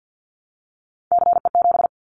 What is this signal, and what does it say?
Morse code, heard as a keyed tone, about 50 wpm, 700 Hz. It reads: CE7
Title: Callsign Prefixes for Top 100 DXCC - 50wpm